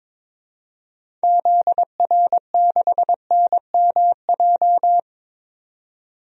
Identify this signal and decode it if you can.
Morse code, heard as a keyed tone, about 22 wpm, 700 Hz. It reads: ZR6NMJ
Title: DXCC-100 Call Signs - 6 Characters - 22wpm